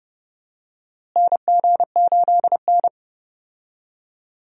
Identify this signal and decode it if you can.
Morse code, heard as a keyed tone, about 30 wpm, 700 Hz. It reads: NG8N
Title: USA Call Signs - 4 Characters - Repeated 2x - 30wpm